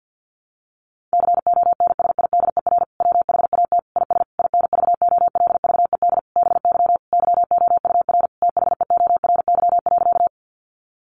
Morse code, heard as a keyed tone, about 50 wpm, 700 Hz. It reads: CONSIDER WHAT IS INVOLVED BY YOUR THEORY.
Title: Sentences from Top 800 Words 50wpm